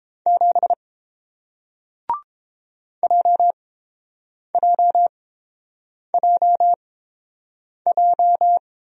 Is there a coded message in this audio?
Yes: Morse code from 0.3 to 8.6 s, heard as a keyed tone, about 31 wpm, spaced as for 7 wpm, 700 Hz. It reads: 7 JJJJ